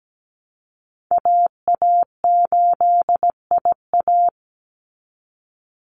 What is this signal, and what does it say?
Morse code, heard as a keyed tone, about 17 wpm, 700 Hz. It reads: AA8IA
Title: Call Signs Repeated 2x - 17wpm